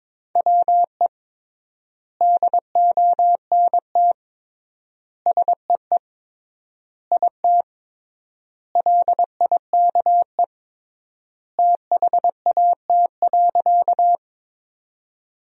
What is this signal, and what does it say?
Morse code, heard as a keyed tone, about 22 wpm, 700 Hz. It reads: WE DONT SEE IT LIKE THAT.
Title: 3x Word-Spacing - Sentences from Top 300 Words - 22wpm